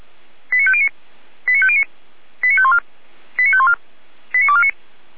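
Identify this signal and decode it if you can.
Five-tone selective call (ZVEI), heard as quick runs of five tones, tones of about 70 ms. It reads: 89509 89509 89514 89514 89249
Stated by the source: ZVEI five-tone selective call received by SDR.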